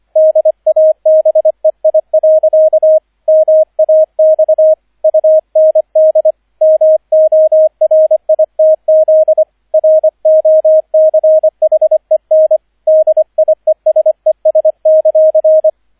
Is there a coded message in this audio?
Yes: Morse code from 0.2 to 15.7 s, heard as a keyed tone, about 25 wpm, 630 Hz, with radio noise. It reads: DABEI. MAX UND MORITZ ROCHEN DIESES;